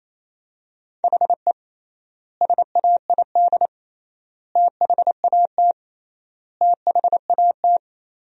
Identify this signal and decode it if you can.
Morse code, heard as a keyed tone, about 28 wpm, 700 Hz. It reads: HE SAID THAT THAT
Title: 3x Word-Spacing - Sentences from Top 200 Words - 28wpm